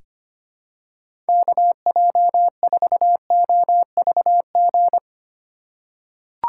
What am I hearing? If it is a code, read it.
Morse code, heard as a keyed tone, about 25 wpm, 700 Hz. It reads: KJ4OVG
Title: USA Call Signs - 2x Character Spacing - 25wpm